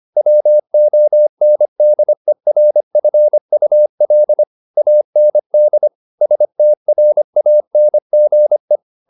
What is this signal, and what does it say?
Morse code, heard as a keyed tone, about 25 wpm, 600 Hz. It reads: WONDERFUL AND STRANGE
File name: the old man and the sea part 1 25wpm